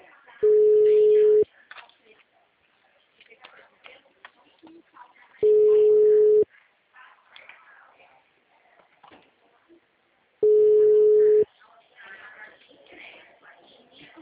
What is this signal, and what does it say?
Ringback tone (Europe, 425 Hz), with speech.